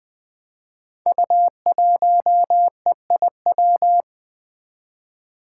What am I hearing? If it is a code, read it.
Morse code, heard as a keyed tone, about 20 wpm, 700 Hz. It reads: U1EIW
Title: Five Challenging Letter-Number Combinations 20wpm